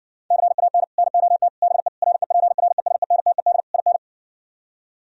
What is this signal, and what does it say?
Morse code, heard as a keyed tone, about 60 wpm, 700 Hz. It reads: CAN NOT BE REPRESENTED IN